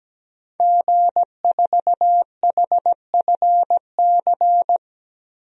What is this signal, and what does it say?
Morse code, heard as a keyed tone, about 17 wpm, 700 Hz. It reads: G4HFC
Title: DXCC-100 Call Signs - 5 Characters - Repeated 3x - 17wpm